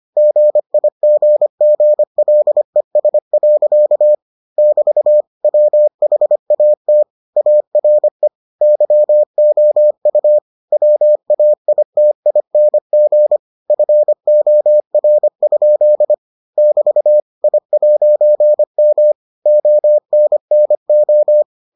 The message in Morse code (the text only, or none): GIGGLES. <BT> WHAT ARE YOU WAITING FOR? <BT> I'M ONNO